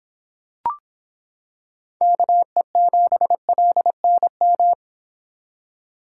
Morse code, heard as a keyed tone, about 25 wpm, 700 Hz. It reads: KE7LNM